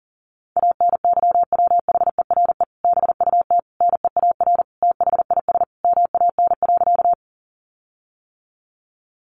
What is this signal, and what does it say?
Morse code, heard as a keyed tone, about 40 wpm, 700 Hz. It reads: ANYWHERE BUT NEAR THIS MAN.
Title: Sentences from Top 1400 Words 40wpm